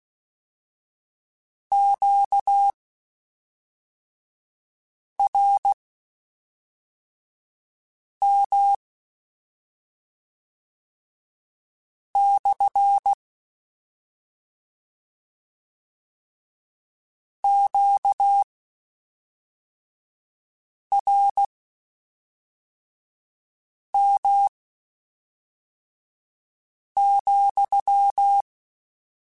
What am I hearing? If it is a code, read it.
Morse code, heard as a keyed tone, about 16 wpm, spaced as for 3 wpm, 800 Hz. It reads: QRM / QRM,